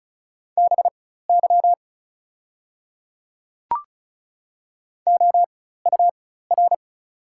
Morse code, heard as a keyed tone, about 36 wpm, spaced as for 16 wpm, 700 Hz. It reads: BY OUR
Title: 4x Character Spacing - Top 200 Words - 35wpm